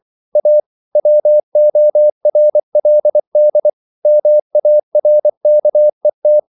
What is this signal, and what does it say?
Morse code, heard as a keyed tone, about 24 wpm, 600 Hz. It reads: A WORLD MARKET
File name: CW / Practice / EnglishQuotations0424